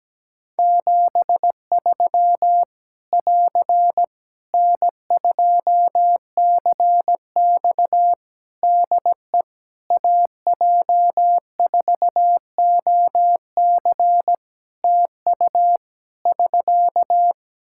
Morse code, heard as a keyed tone, about 17 wpm, 700 Hz. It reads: 73 <AR> N2CX DE AJ4OC TU <SK>
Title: Practice QSO - Exchange 4 - Ending - 17wpm